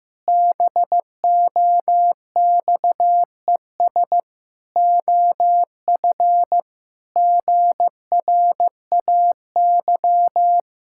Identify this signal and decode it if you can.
Morse code, heard as a keyed tone, about 16 wpm, 700 Hz. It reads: BOXES OF GRAY
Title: Pangram Challenge v2 - 15wpm